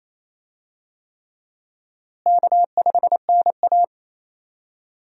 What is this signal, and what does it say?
Morse code, heard as a keyed tone, about 27 wpm, 700 Hz. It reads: K5NA
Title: USA Call Signs - 4 Characters - Repeated 3x - 28wpm